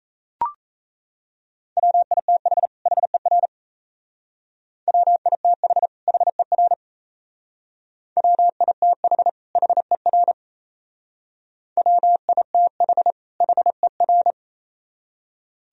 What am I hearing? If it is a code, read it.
Morse code, heard as a keyed tone, about 37 wpm, 700 Hz. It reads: WITH HER WITH HER WITH HER WITH HER